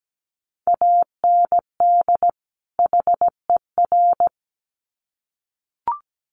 Morse code, heard as a keyed tone, about 17 wpm, 700 Hz. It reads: AND HER
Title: Sets of 2 Words 17wpm